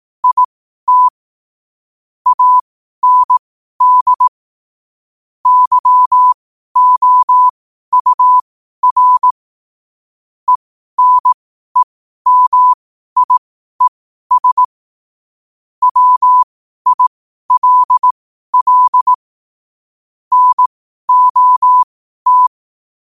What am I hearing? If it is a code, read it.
Morse code, heard as a keyed tone, about 18 wpm, spaced as for 12 wpm, 1000 Hz. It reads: IT AND YOUR ENEMIES WILL NOT
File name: qotdfiles / 12wpm